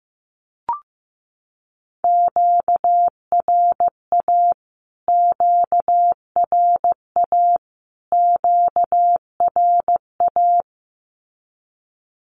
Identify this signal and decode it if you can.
Morse code, heard as a keyed tone, about 15 wpm, 700 Hz. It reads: QRA QRA QRA
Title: Q-Codes Repeated 3x 15wpm